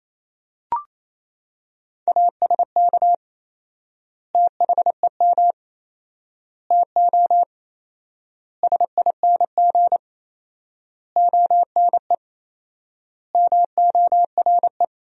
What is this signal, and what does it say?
Morse code, heard as a keyed tone, about 28 wpm, 700 Hz. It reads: ASK THEM TO SING ONE MORE
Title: 4x Word-Spacing - Sentences from Top 400 Words - 28wpm